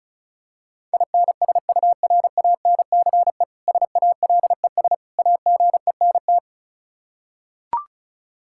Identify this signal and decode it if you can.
Morse code, heard as a keyed tone, about 36 wpm, 700 Hz. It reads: INSURANCE SALES AGENT
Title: K1USN SST QSO Practice 35wpm